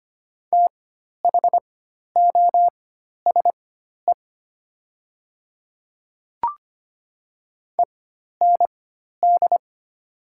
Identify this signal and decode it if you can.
Morse code, heard as a keyed tone, about 25 wpm, spaced as for 12 wpm, 700 Hz. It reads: THOSE END
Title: 4x Character Spacing - Top 300 Words - 25wpm